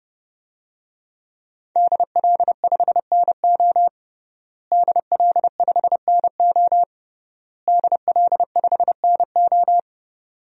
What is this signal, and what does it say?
Morse code, heard as a keyed tone, about 32 wpm, 700 Hz. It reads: DL5NO DL5NO DL5NO
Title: DXCC-100 Call Signs - 5 Characters - Repeated 3x - 30wpm